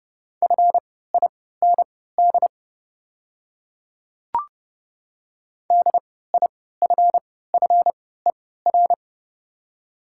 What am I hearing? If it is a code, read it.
Morse code, heard as a keyed tone, about 30 wpm, spaced as for 17 wpm, 690 Hz. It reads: FIND DIFFER
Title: 3x Character Spacing - Top 200 Words - 30wpm